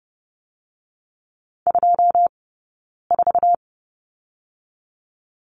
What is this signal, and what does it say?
Morse code, heard as a keyed tone, about 30 wpm, 700 Hz. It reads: 2 4
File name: Lesson 18d - Letter m - Two Characters 30wpm